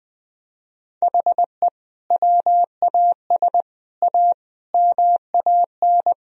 Morse code, heard as a keyed tone, about 21 wpm, 700 Hz. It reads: HE WAS A MAN
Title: Sentences from Top 1000 Words - Review Part 3 - 20wpm